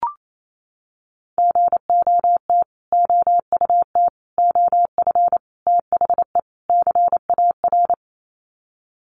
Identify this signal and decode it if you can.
Morse code, heard as a keyed tone, about 28 wpm, 700 Hz. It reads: GOT OUT OF THE CAR
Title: Sets of 5 Words Encore 28wpm